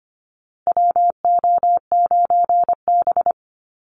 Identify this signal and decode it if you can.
Morse code, heard as a keyed tone, about 25 wpm, 700 Hz. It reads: WO9B
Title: Call Signs Repeated 3x - 25wpm